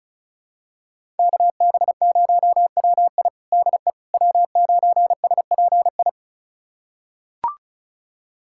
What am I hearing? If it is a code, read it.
Morse code, heard as a keyed tone, about 36 wpm, 700 Hz. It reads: KB0WI DE W9SPI
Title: USA Callsign DE Callsign - 5 Characters - 35wpm